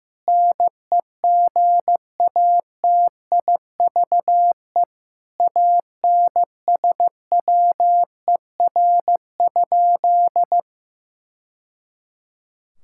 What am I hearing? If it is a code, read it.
Morse code, heard as a keyed tone, about 15 wpm, 700 Hz. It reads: NEGATIVE ANSWER?
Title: Sentences from Top 1000 Words - Review Part 6 - 15wpm